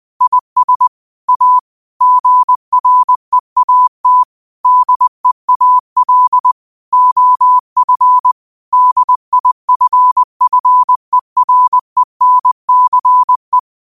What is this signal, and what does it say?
Morse code, heard as a keyed tone, about 20 wpm, 1000 Hz. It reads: IS A GREAT DEAL OF DIFFERENCE
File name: qotdfiles / 20wpm